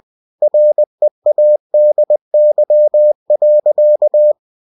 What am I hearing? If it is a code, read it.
Morse code, heard as a keyed tone, about 20 wpm, 600 Hz. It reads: READY.
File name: cw-books / winnie-the-pooh / winnie-20wpm-page-0071